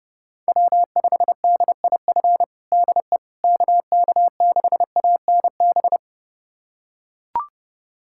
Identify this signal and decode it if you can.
Morse code, heard as a keyed tone, about 30 wpm, 700 Hz. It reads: W5DIF DE KK6ANB
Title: USA Callsign DE Callsign 30wpm